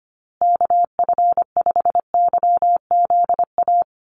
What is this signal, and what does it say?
Morse code, heard as a keyed tone, about 25 wpm, 700 Hz. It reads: KF5YZA
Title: USA Call Signs Repeated 3x 25wpm